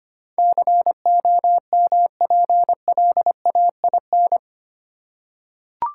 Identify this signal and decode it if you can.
Morse code, heard as a keyed tone, about 25 wpm, 700 Hz. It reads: COMPLAIN
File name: Eight Letter Words - 25wpm